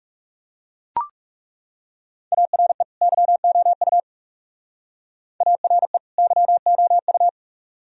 Morse code, heard as a keyed tone, about 35 wpm, 700 Hz. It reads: ARE YOU ARE YOU
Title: Speed-Racing - Sets of 2 Words 30wpm